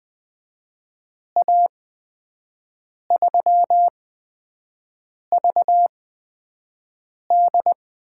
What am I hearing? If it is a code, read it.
Morse code, heard as a keyed tone, about 20 wpm, spaced as for 5 wpm, 700 Hz. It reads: A3VD
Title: USA Call Signs - 6 Characters - 8x Character Spacing - 20wpm